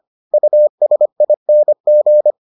Morse code, heard as a keyed tone, about 25 wpm, 600 Hz. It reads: USING